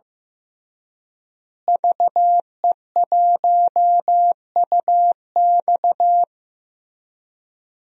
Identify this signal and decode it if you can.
Morse code, heard as a keyed tone, about 15 wpm, 700 Hz. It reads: VE1UX
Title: DXCC-100 Call Signs - 5 Characters - Repeated 3x - 15wpm